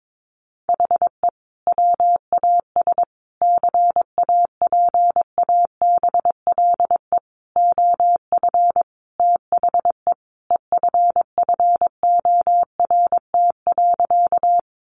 Morse code, heard as a keyed tone, about 22 wpm, 700 Hz. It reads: HE WAS CAPABLE OF THE EFFORT.
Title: Sentences from Top 1200 Words 22wpm